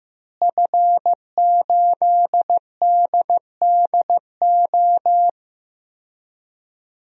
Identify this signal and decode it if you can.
Morse code, heard as a keyed tone, about 15 wpm, 700 Hz. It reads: F8DDO
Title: DXCC-100 Call Signs - 5 Characters - Repeated 3x - 15wpm